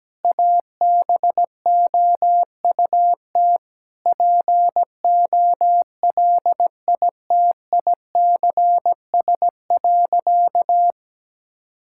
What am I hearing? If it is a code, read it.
Morse code, heard as a keyed tone, about 17 wpm, 700 Hz. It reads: ABOUT POLITICS.
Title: Sentences from Top 1000 Words - Review Part 6 - 17wpm